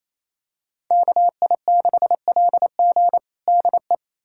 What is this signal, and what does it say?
Morse code, heard as a keyed tone, about 28 wpm, 700 Hz. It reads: KI6LG DE
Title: USA Callsign DE Callsign - 5 Characters - 28wpm